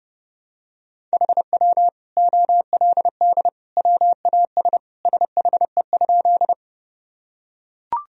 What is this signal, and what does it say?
Morse code, heard as a keyed tone, about 32 wpm, 700 Hz. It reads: HW OLD WAS SHE?
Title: Sentences from Top 200 Words and Abbreviations 30wpm